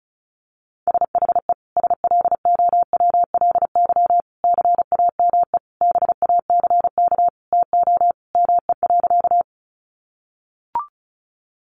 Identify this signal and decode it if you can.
Morse code, heard as a keyed tone, about 34 wpm, 700 Hz. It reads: SHE SLOWLY CAME BACK TO ME.
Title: Sentences from Top 1100 Words 35wpm